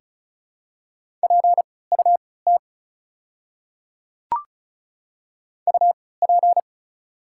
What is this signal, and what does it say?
Morse code, heard as a keyed tone, about 35 wpm, spaced as for 20 wpm, 700 Hz. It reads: PUT UP
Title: 3x Character Spacing - Top 200 Words - 35wpm